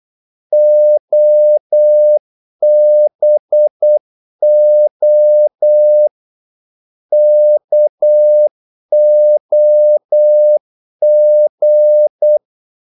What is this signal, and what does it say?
Morse code, heard as a keyed tone, about 8 wpm, 600 Hz. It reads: OBO KOG
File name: Korte ord DK 8wpm